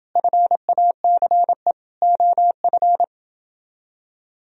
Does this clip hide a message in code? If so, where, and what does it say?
Yes: Morse code from 0.2 to 3.0 s, heard as a keyed tone, about 29 wpm, 700 Hz. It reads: FACE OF